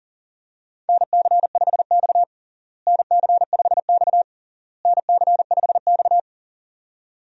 Morse code, heard as a keyed tone, about 37 wpm, 700 Hz. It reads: NC5X NC5X NC5X
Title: USA Call Signs - 4 Characters - Repeated 3x - 40wpm